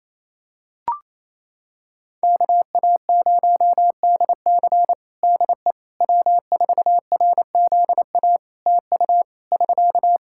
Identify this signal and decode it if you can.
Morse code, heard as a keyed tone, about 28 wpm, 700 Hz. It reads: KA0DC DE W4RZA TU <SK>
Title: Practice QSO - Exchange 4 - Ending - 28wpm